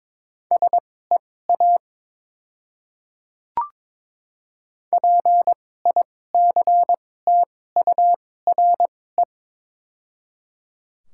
Morse code, heard as a keyed tone, about 21 wpm, spaced as for 16 wpm, 700 Hz. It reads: SEA PICTURE
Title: 2x Character Spacing - Top 300 Words - 22wpm